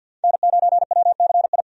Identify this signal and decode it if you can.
Morse code, heard as a keyed tone, about 51 wpm, 700 Hz. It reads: N9WCI
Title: USA Call Signs - 5 Characters - 50wpm